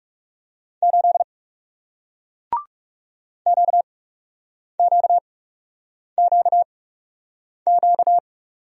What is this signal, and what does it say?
Morse code, heard as a keyed tone, about 40 wpm, spaced as for 8 wpm, 700 Hz. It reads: 8 QQQQ